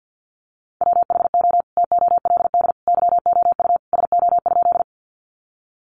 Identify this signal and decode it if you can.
Morse code, heard as a keyed tone, about 50 wpm, 700 Hz. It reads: WHO TOLD YOU SO?
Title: Sentences from Top 400 Words 50wpm